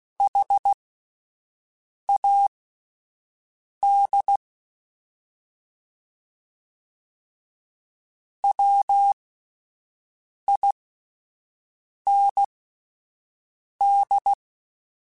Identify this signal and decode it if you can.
Morse code, heard as a keyed tone, about 16 wpm, spaced as for 5 wpm, 800 Hz. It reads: HAD WIND
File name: Spaced Characters / Spaced Characters-mp3s / Weather5 5 sp @10